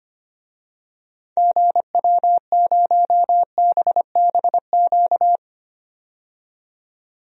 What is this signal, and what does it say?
Morse code, heard as a keyed tone, about 25 wpm, 700 Hz. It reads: GW0BBQ
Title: DXCC-100 Call Signs - 6 Characters - 25wpm